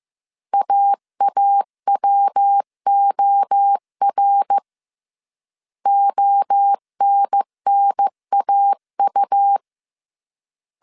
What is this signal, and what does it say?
Morse code, heard as a keyed tone, about 15 wpm, 800 Hz. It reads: AAWOR ONNAU
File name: CW / DET / 1 3 1500C S A 14wpm 2447